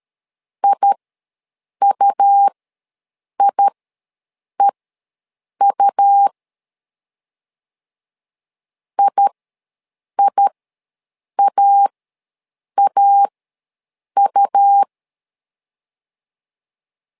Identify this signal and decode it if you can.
Morse code, heard as a keyed tone, about 13 wpm, spaced as for 7 wpm, 800 Hz. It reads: IUIEU IIAAU
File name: LEZIONI / 2 UHDN / 2 U ws 12wpm 1655